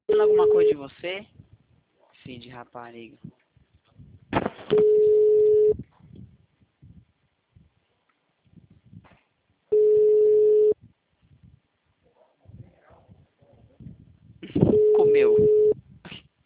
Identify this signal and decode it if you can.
Ringback tone (Europe, 425 Hz), with speech.
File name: out- -2004- - - .56318